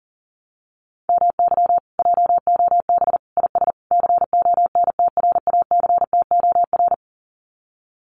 Morse code, heard as a keyed tone, about 40 wpm, 700 Hz. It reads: MY JOB IS CONTRACTOR
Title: Common QSO Phrases Encore 40wpm